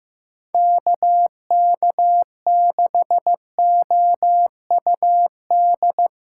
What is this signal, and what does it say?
Morse code, heard as a keyed tone, about 15 wpm, 700 Hz. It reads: KK6OUD
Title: USA Call Signs - 6 Characters - 3x Character Spacing - 15wpm